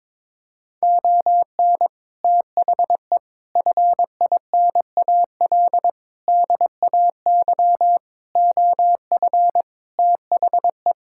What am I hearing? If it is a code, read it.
Morse code, heard as a keyed tone, about 22 wpm, 700 Hz. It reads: ON THE FINAL DAY OF THE
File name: Sentences with 7-Word Phrases - 22wpm